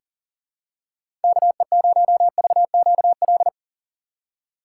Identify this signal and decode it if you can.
Morse code, heard as a keyed tone, about 40 wpm, 700 Hz. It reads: KE0VQL
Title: USA Call Signs - 6 Characters - 3x Character Spacing - 40wpm